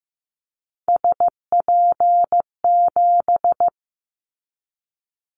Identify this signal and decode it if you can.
Morse code, heard as a keyed tone, about 15 wpm, 700 Hz. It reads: SP7
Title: Callsign Prefixes for Top 25 DXCC - 15wpm